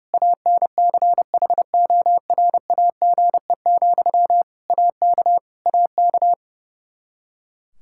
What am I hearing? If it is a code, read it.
Morse code, heard as a keyed tone, about 30 wpm, 700 Hz. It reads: ANCHORAGE, AK AK
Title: Practice QSO - Exchange 1 - Beginning - 30wpm